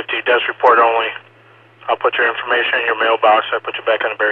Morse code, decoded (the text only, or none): none